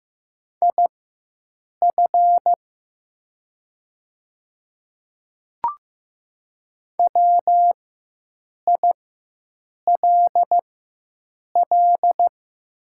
Morse code, heard as a keyed tone, about 15 wpm, spaced as for 7 wpm, 700 Hz. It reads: IF WILL
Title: 4x Character Spacing - Top 100 Words - 15wpm